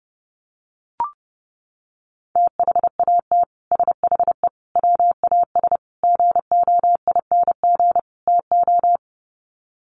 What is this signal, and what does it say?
Morse code, heard as a keyed tone, about 30 wpm, 700 Hz. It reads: THAT SHE WAS GOING TO
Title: Sets of 5 Words Encore2 30wpm